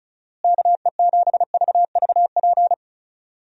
Morse code, heard as a keyed tone, about 34 wpm, 700 Hz. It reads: KE7VVP
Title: USA Call Signs - 6 Characters - Repeated 3x - 35wpm